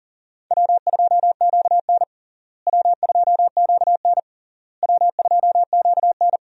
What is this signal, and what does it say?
Morse code, heard as a keyed tone, about 40 wpm, 700 Hz. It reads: W2QN W2QN W2QN
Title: USA Call Signs - 4 Characters - Repeated 3x - 40wpm